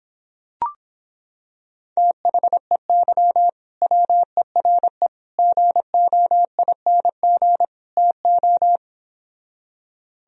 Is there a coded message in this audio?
Yes: Morse code from 2.0 to 8.8 s, heard as a keyed tone, about 26 wpm, 700 Hz. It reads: THEY WERE GOING TO